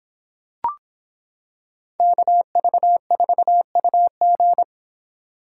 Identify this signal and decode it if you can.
Morse code, heard as a keyed tone, about 26 wpm, 700 Hz. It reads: KV4UG